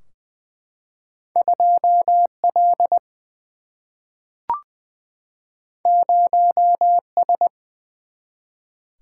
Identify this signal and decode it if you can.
Morse code, heard as a keyed tone, about 20 wpm, 700 Hz. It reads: 2L 0S
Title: Single-Letter Single-Number 20wpm